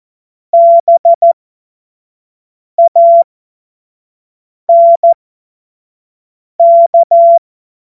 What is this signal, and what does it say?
Morse code, heard as a keyed tone, about 14 wpm, spaced as for 5 wpm, 680 Hz. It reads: BANK
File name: Morsekurs / Witze / Witz 29 April 2025 25 F70